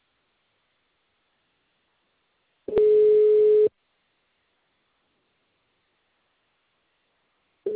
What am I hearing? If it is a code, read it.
Ringback tone (Europe, 425 Hz).